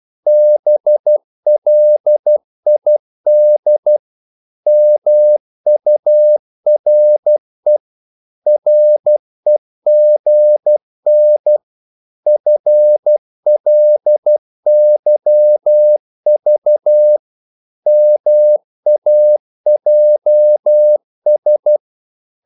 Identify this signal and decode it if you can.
Morse code, heard as a keyed tone, about 12 wpm, 600 Hz. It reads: BLID MURE REGN FLYV MAJS
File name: Korte ord DK 12wpm